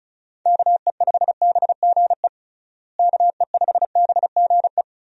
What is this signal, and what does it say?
Morse code, heard as a keyed tone, about 36 wpm, 700 Hz. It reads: KE5BGE KE5BGE
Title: USA Call Signs - 6 Characters - Repeated 2x - 35wpm